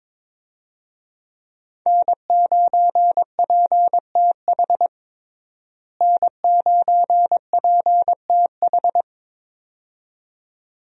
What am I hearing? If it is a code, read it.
Morse code, heard as a keyed tone, about 21 wpm, 700 Hz. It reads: N9PTH N9PTH
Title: USA Call Signs - 5 Characters - Repeated 2x - 22wpm